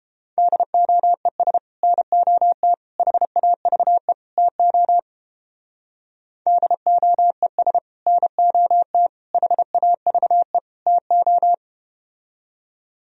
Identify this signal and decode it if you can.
Morse code, heard as a keyed tone, about 33 wpm, 700 Hz. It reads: DOES NOT HAVE TO DOES NOT HAVE TO